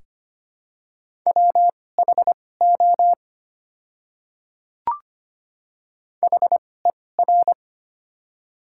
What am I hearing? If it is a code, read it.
Morse code, heard as a keyed tone, about 25 wpm, spaced as for 18 wpm, 700 Hz. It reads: WHO HER
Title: 2x Character Spacing - Top 200 Words - 25wpm